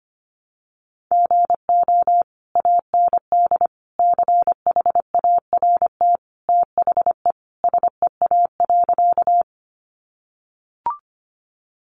Morse code, heard as a keyed tone, about 25 wpm, 700 Hz. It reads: GO AND CHART THE SEA.
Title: Sentences from Top 1300 Words 25wpm